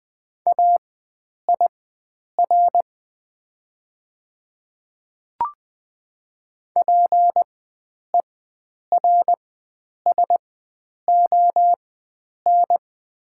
Morse code, heard as a keyed tone, about 20 wpm, spaced as for 9 wpm, 700 Hz. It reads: AIR PERSON
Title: 4x Character Spacing - Top 500 Words - 20wpm